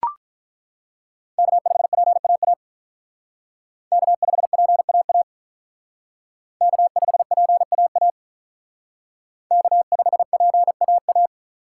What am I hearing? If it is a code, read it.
Morse code, heard as a keyed tone, about 42 wpm, 700 Hz. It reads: K5PAA K5PAA K5PAA K5PAA
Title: Speed Racing - USA 1x3 Call Signs 35wpm